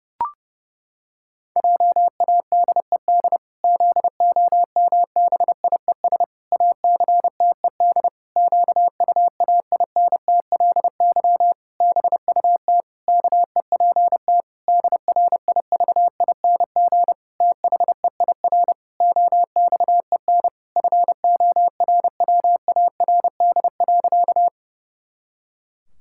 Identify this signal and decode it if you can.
Morse code, heard as a keyed tone, about 32 wpm, 700 Hz. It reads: JADED ZOMBIES ACTED QUAINTLY BUT KEPT DRIVING THEIR OXEN FORWARD.
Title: Pangram Challenge v2 - 30wpm